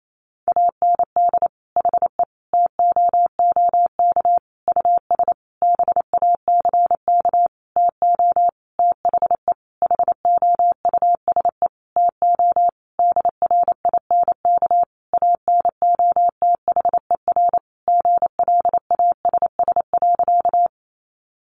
Morse code, heard as a keyed tone, about 28 wpm, 700 Hz. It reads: AND HE TOOK US BACK TO THE HOUSE TO DRINK ANOTHER GLASS.
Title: Sentences from Top 1500 Words 28wpm